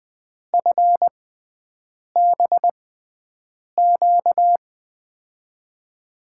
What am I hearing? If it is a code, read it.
Morse code, heard as a keyed tone, about 20 wpm, spaced as for 7 wpm, 700 Hz. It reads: FBQ
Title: USA Call Signs - 6x Character Spacing - 20wpm